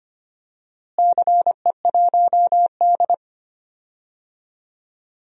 Morse code, heard as a keyed tone, about 25 wpm, 700 Hz. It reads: CE1D
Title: DXCC-100 Call Signs - 4 Characters - Repeated 3x - 25wpm